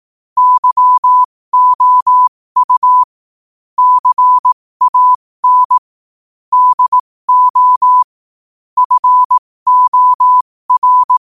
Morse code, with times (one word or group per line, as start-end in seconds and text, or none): 0.4-3.0 YOU
3.8-5.8 CAN
6.5-8.0 DO
8.8-11.2 FOR